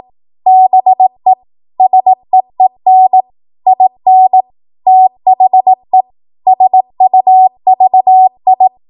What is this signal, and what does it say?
Morse code, heard as a keyed tone, about 18 wpm, 750 Hz. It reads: BE SEEN IN THE SUVI
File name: Morse / 18WPM